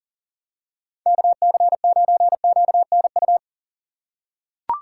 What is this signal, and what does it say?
Morse code, heard as a keyed tone, about 40 wpm, 700 Hz. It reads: KC9QNU
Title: USA Call Signs - 6 Characters - 2x Character Spacing - 40wpm